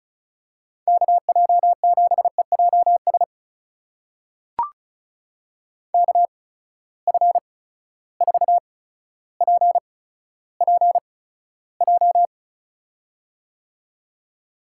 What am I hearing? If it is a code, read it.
Morse code, heard as a keyed tone, about 36 wpm, spaced as for 15 wpm, 700 Hz. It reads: KJ7EJS KF4PPJ
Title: USA Call Signs - 6 Characters - 8x Character Spacing - 35wpm